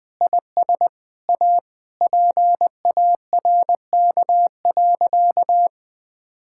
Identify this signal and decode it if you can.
Morse code, heard as a keyed tone, about 20 wpm, 700 Hz. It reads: IS A PARK.
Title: Sentences with 5-Word Phrases - 20wpm